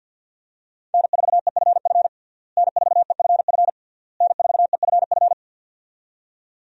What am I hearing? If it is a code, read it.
Morse code, heard as a keyed tone, about 51 wpm, 700 Hz. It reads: N4EFF N4EFF N4EFF
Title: USA Call Signs - 5 Characters - Repeated 3x - 50wpm